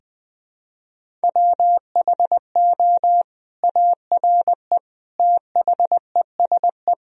Morse code, heard as a keyed tone, about 21 wpm, 700 Hz. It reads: WHO ARE THESE
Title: Sentences from Top 1000 Words - Review Part 1 - 20wpm